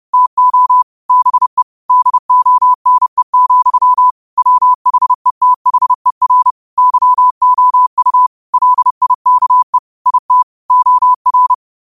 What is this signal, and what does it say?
Morse code, heard as a keyed tone, about 30 wpm, 1000 Hz. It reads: TO BE DONE, WHETHER YOU LIKE IT OR
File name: qotdfiles / 30wpm